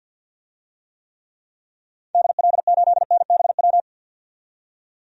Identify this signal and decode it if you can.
Morse code, heard as a keyed tone, about 52 wpm, 700 Hz. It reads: DL8NBW
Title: DXCC-100 Call Signs - 6 Characters - 50wpm